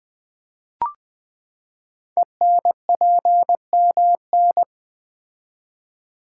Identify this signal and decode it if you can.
Morse code, heard as a keyed tone, about 20 wpm, 700 Hz. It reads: ENPMN
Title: Code Groups - 5 Letter:Numbers - 20wpm